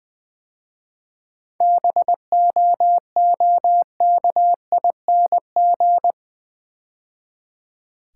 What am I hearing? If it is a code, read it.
Morse code, heard as a keyed tone, about 20 wpm, 700 Hz. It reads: BOOKING
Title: 7 Letter Words - First 3 Letters Given - 20wpm